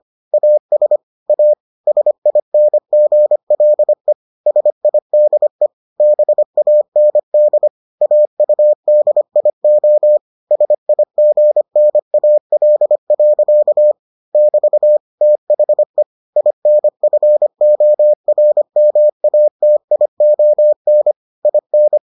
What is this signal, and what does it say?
Morse code, heard as a keyed tone, about 25 wpm, 600 Hz. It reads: AS A SINGLE SIDE BAND AUDIO SIGNAL. <BT> THE INFORMATION IN